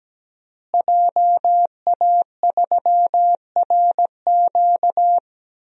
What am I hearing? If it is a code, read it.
Morse code, heard as a keyed tone, about 17 wpm, 700 Hz. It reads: JA3RQ
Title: DXCC-100 Call Signs - 5 Characters - Repeated 3x - 17wpm